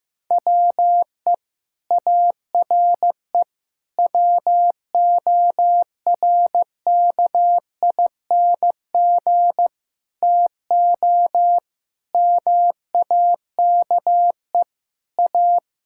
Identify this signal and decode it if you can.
Morse code, heard as a keyed tone, about 15 wpm, 700 Hz. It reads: WE ARE WORKING TO MAKE A
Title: Sentences from Top 1000 Words - Review Part 6 - 15wpm